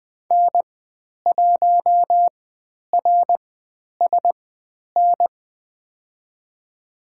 Morse code, heard as a keyed tone, about 20 wpm, spaced as for 10 wpm, 700 Hz. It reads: N1RSN
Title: 109 Lesson 18g - Letter m - Call Signs 20wpm